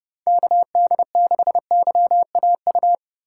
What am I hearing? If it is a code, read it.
Morse code, heard as a keyed tone, about 32 wpm, 700 Hz. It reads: KD6YAU
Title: USA Call Signs - 6 Characters - 6x Character Spacing - 30wpm